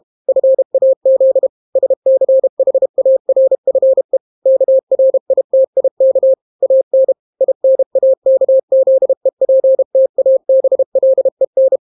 Morse code, heard as a keyed tone, about 32 wpm, 530 Hz. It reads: FAZ SCHARFE KRITIK AN INAKZEPTABLEN